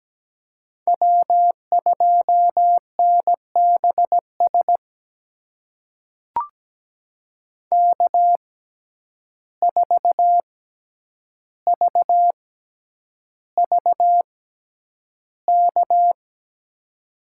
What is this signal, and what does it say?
Morse code, heard as a keyed tone, about 17 wpm, spaced as for 9 wpm, 700 Hz. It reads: W2NBS K4VVK
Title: USA Call Signs - 5 Characters - 6x Character Spacing - 17wpm